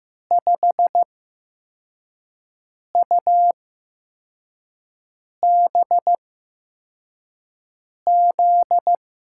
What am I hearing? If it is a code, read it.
Morse code, heard as a keyed tone, about 15 wpm, spaced as for 4 wpm, 700 Hz. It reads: 5UBZ
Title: USA Call Signs - 8x Character Spacing - 15wpm